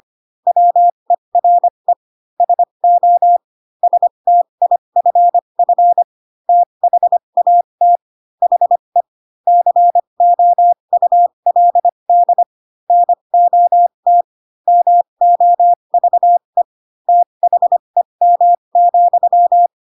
Morse code, heard as a keyed tone, about 25 wpm, spaced as for 22 wpm, 700 Hz. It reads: WERE SO STIFF THAT HE COULD NOT MOVE THEM,
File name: Pinocchio / Pinocchio ch3 18 25 ik1hge